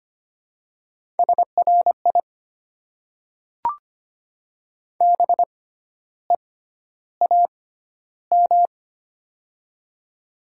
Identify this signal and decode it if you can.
Morse code, heard as a keyed tone, about 25 wpm, spaced as for 12 wpm, 700 Hz. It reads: SRI BEAM
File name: Top 90 QSO Elements - 6x Character Spacing - 25wpm